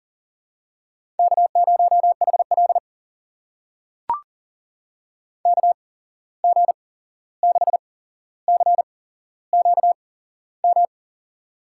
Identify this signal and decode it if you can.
Morse code, heard as a keyed tone, about 40 wpm, spaced as for 15 wpm, 700 Hz. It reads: K0HL KG6CQM
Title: USA Call Signs - 8x Character Spacing - 40wpm